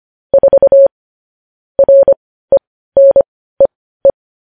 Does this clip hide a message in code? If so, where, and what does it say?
Yes: Morse code from 0.3 to 4.1 s, heard as a keyed tone, about 25 wpm, spaced as for 15 wpm, 550 Hz. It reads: 4 RENEE